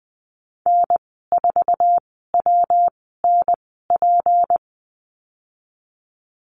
Morse code, heard as a keyed tone, about 20 wpm, spaced as for 14 wpm, 700 Hz. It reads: N4WNP
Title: 179 Lesson 29g - Slash - Call Signs 20wpm